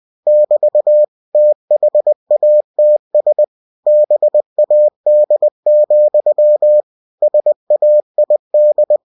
Morse code, heard as a keyed tone, about 20 wpm, 600 Hz. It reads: <BT> THATS BAD, SAID
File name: cw-books / winnie-the-pooh / winnie-20wpm-page-0076